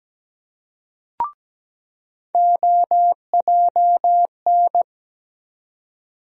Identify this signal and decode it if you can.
Morse code, heard as a keyed tone, about 17 wpm, 700 Hz. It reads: OJN
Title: USA Call Sign Postfixes - 3 Letters 17wpm